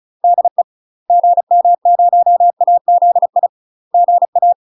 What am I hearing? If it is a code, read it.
Morse code, heard as a keyed tone, about 34 wpm, 700 Hz. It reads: DE GM0AZI GA
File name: UK MASTER QSOs TRAINING / 35WPM / UK MASTER QSOs 0002